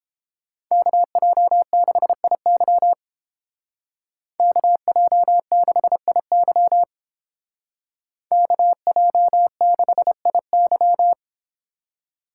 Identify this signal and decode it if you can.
Morse code, heard as a keyed tone, about 29 wpm, 700 Hz. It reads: KJ6IY KJ6IY KJ6IY